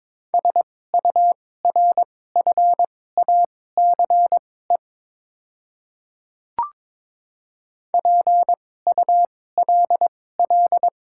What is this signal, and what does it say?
Morse code, heard as a keyed tone, about 22 wpm, spaced as for 16 wpm, 690 Hz. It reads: SURFACE PULL
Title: 2x Character Spacing - Top 500 Words - 22wpm